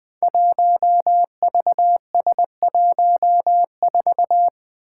Morse code, heard as a keyed tone, about 20 wpm, 700 Hz. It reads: 1VS14
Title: Five Challenging Letter-Number Combinations 20wpm